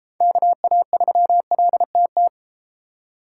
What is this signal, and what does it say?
Morse code, heard as a keyed tone, about 32 wpm, 700 Hz. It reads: KA3LTT